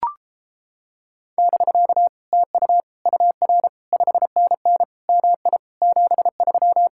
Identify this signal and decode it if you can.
Morse code, heard as a keyed tone, about 29 wpm, 700 Hz. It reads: <BK> TU UR 5NN MI 73